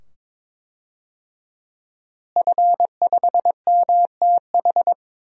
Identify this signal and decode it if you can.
Morse code, heard as a keyed tone, about 22 wpm, 700 Hz. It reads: F5MTH
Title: DXCC-100 Call Signs - 5 Characters - Repeated 2x - 22wpm